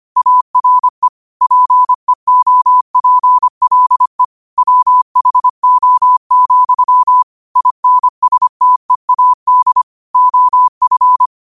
Morse code, heard as a keyed tone, about 25 wpm, 1000 Hz. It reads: ARE PEOPLE WHO, INSTEAD OF
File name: qotdfiles / 25wpm